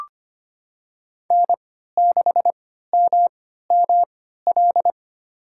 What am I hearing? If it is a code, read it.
Morse code, heard as a keyed tone, about 25 wpm, spaced as for 14 wpm, 700 Hz. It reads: N6MML
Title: USA Call Signs - 5 Characters - 3x Character Spacing - 25wpm